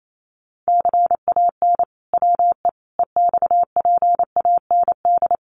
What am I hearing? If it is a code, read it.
Morse code, heard as a keyed tone, about 28 wpm, 700 Hz. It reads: CAN WE EXPAND
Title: Sentences from Top 1400 Words 28wpm